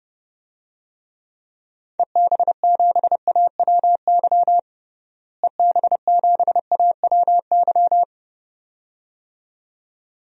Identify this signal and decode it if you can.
Morse code, heard as a keyed tone, about 30 wpm, 700 Hz. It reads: EB7AWY EB7AWY
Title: DXCC-100 Call Signs - 6 Characters - Repeated 2x - 30wpm